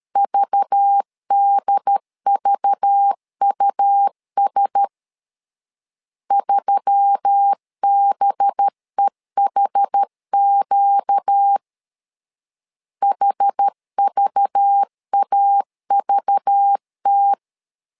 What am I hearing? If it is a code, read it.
Morse code, heard as a keyed tone, about 13 wpm, 800 Hz. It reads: VDVUS 3BEHQ HVAVT
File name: LEZIONI / 5 LBVC / 5 LBV 12wpm 1855